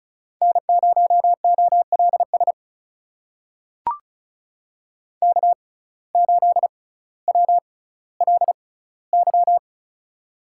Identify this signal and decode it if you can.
Morse code, heard as a keyed tone, about 36 wpm, spaced as for 18 wpm, 700 Hz. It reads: N0OLS K8WLY
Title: USA Call Signs - 5 Characters - 6x Character Spacing - 35wpm